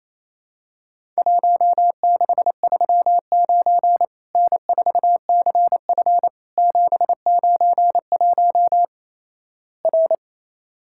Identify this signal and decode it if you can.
Morse code, heard as a keyed tone, about 27 wpm, 700 Hz. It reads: 1639 N4CF 791 R
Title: Worked All Europe DX Contest - QTC Exchange - 28wpm